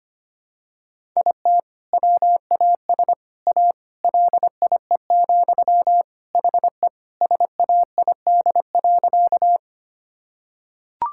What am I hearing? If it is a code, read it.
Morse code, heard as a keyed tone, about 25 wpm, 700 Hz. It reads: IT WAS A LIE, HE SAID.
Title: Sentences from Top 1600 Words 25wpm